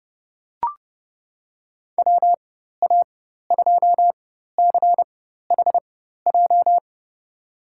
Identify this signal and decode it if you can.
Morse code, heard as a keyed tone, about 30 wpm, spaced as for 14 wpm, 700 Hz. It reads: WA2CHJ
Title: USA Call Signs - 6 Characters - 4x Character Spacing - 30wpm